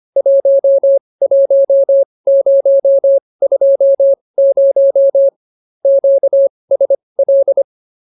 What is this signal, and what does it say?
Morse code, heard as a keyed tone, about 24 wpm, spaced as for 20 wpm, 550 Hz. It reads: 11020 QSL